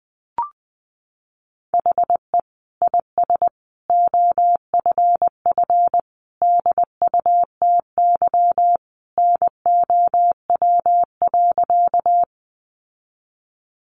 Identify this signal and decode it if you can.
Morse code, heard as a keyed tone, about 20 wpm, 700 Hz. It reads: HE IS OFF DUTY NOW.
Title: Sentences from Top 1500 Words 20wpm